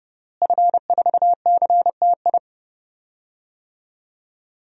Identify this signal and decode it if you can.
Morse code, heard as a keyed tone, about 30 wpm, 700 Hz. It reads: F4CTI
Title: DXCC-100 Call Signs - 5 Characters - Repeated 3x - 30wpm